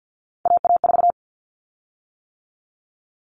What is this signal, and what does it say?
Morse code, heard as a keyed tone, about 50 wpm, 700 Hz. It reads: AA4
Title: Common Cut Numbers - 3-Digit 50wpm